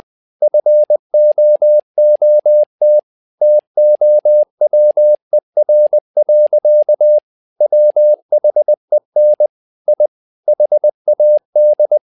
Morse code, heard as a keyed tone, about 20 wpm, 600 Hz. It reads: FOOT TOWER. WHEN I HAD
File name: learning center / tower trouble / tower trouble 20wpm